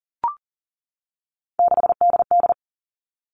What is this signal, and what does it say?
Morse code, heard as a keyed tone, about 40 wpm, 700 Hz. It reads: 6DD
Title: Three Challenging Letter-Number Combinations 40wpm